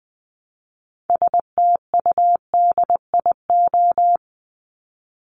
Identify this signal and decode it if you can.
Morse code, heard as a keyed tone, about 20 wpm, 700 Hz. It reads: STUDIO
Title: Top 1001-1250 Words 20wpm